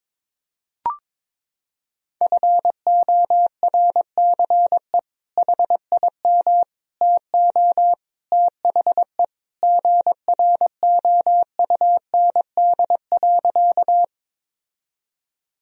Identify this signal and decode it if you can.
Morse code, heard as a keyed tone, about 22 wpm, 700 Hz. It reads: FORCE HIM TO THE GROUND.
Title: Sentences from Top 1000 Words - Review Part 3 - 22wpm